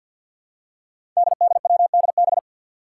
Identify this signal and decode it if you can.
Morse code, heard as a keyed tone, about 51 wpm, 700 Hz. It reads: DDWDB
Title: Five Challenging Letter-Number Combinations 50wpm